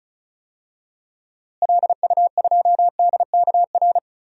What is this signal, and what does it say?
Morse code, heard as a keyed tone, about 36 wpm, 700 Hz. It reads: LU2DKR
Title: DXCC-100 Call Signs - 6 Characters - Repeated 2x - 35wpm